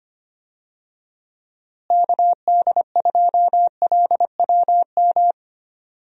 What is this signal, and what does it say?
Morse code, heard as a keyed tone, about 25 wpm, 700 Hz. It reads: KD2LWM
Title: USA Call Signs - 6 Characters - Repeated 3x - 25wpm